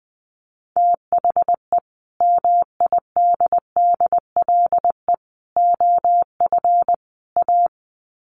Morse code, heard as a keyed tone, about 20 wpm, 700 Hz. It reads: THE MIDDLE OF A
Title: Sets of 4 Words 20wpm